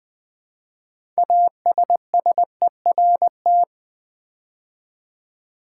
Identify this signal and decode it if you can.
Morse code, heard as a keyed tone, about 20 wpm, 700 Hz. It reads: ASSERT
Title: Build A Word - 2 Letters At A Time - 6-Letter Words - 20wpm